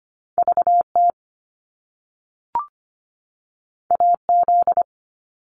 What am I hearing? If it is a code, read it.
Morse code, heard as a keyed tone, about 25 wpm, 700 Hz. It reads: VT AZ
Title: US States 25wpm